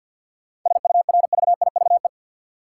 Morse code, heard as a keyed tone, about 51 wpm, 700 Hz. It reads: SURVIVE
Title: Sentences from Top 1300 Words - New Word Drill - 50wpm